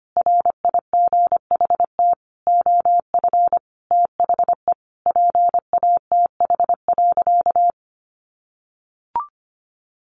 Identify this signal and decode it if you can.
Morse code, heard as a keyed tone, about 25 wpm, 700 Hz. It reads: RIGHT OF THE PATH.
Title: Sentences from Top 1400 Words 25wpm